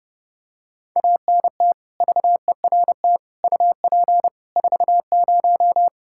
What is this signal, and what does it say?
Morse code, heard as a keyed tone, about 31 wpm, 700 Hz. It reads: ANT VERT UP 40
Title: Practice QSO - Exchange 2 - Info - 30wpm